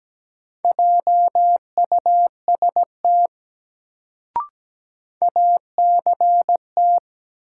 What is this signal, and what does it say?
Morse code, heard as a keyed tone, about 17 wpm, 700 Hz. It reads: JUST ACT
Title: Sentences from Top 200 Words - New Word Drill - 17wpm